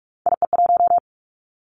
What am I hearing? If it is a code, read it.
Morse code, heard as a keyed tone, about 46 wpm, 700 Hz. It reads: IE1
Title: Three Challenging Letter-Number Combinations 45wpm